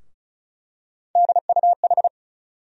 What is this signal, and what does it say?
Morse code, heard as a keyed tone, about 36 wpm, 700 Hz. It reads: DUH
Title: USA Call Sign Postfixes - 3 Letters 35wpm